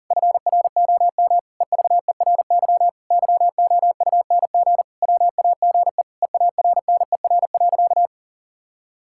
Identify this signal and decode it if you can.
Morse code, heard as a keyed tone, about 37 wpm, 700 Hz. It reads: FROM EVERY YOUNG WAGE EARNER.
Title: Pangram Challenge v2 - 40wpm